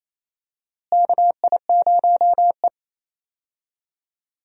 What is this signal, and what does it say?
Morse code, heard as a keyed tone, about 28 wpm, 700 Hz. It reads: KI0E
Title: USA Call Signs - 4 Characters - Repeated 3x - 28wpm